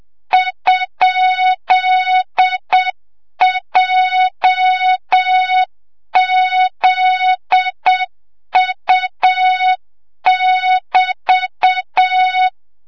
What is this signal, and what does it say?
Morse code, heard as a keyed tone, about 7 wpm, 3700 Hz, with noise. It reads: ?JZU<BT>
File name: Morze-HA2MN / hang / 25 GYAKORLAT